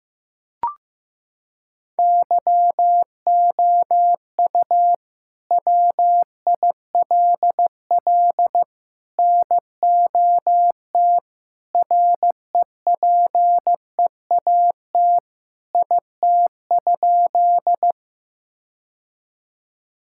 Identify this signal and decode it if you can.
Morse code, heard as a keyed tone, about 15 wpm, 700 Hz. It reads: YOU WILL NOT REPEAT IT?
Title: Sentences from Top 1600 Words 15wpm